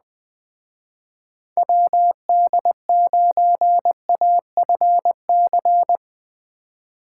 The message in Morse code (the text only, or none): WD9AFC